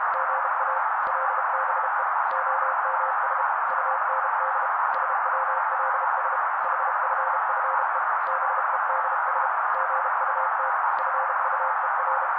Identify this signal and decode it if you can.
Morse code, heard as a keyed tone, about 32 wpm, 530 Hz. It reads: GA LBE OMS UND SWLS HFFE BEDS GUT FUER
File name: daily / 31wpm noise